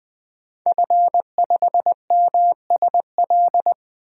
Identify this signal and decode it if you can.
Morse code, heard as a keyed tone, about 20 wpm, 700 Hz. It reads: F5MSL
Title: DXCC-100 Call Signs - 5 Characters - 20wpm